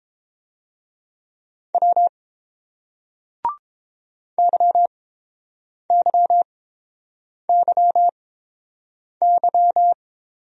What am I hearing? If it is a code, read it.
Morse code, heard as a keyed tone, about 29 wpm, spaced as for 7 wpm, 700 Hz. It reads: W YYYY